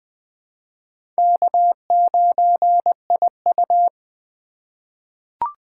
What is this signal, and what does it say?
Morse code, heard as a keyed tone, about 21 wpm, 700 Hz. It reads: K9IU
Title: USA Call Signs - 4 Characters - 4x Character Spacing - 20wpm